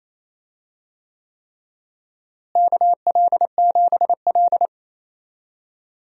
Morse code, heard as a keyed tone, about 28 wpm, 700 Hz. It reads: KL7L
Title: DXCC-100 Call Signs - 4 Characters - Repeated 3x - 28wpm